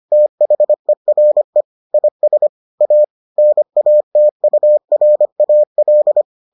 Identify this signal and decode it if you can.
Morse code, heard as a keyed tone, about 25 wpm, 600 Hz. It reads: THERE IS A NATURAL